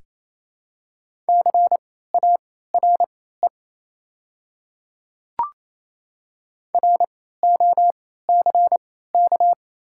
Morse code, heard as a keyed tone, about 28 wpm, spaced as for 16 wpm, 690 Hz. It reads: CARE ROCK
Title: 3x Character Spacing - Top 500 Words - 28wpm